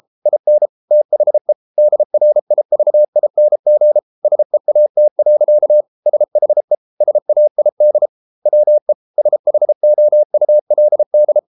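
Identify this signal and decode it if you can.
Morse code, heard as a keyed tone, about 33 wpm, 600 Hz. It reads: IN THE DRIVING SEAT. SHE SAID WE SHOULD
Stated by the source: Morse code practice at 32 wpm from CW med Gnister, reading Ken Follett's novel Never in English.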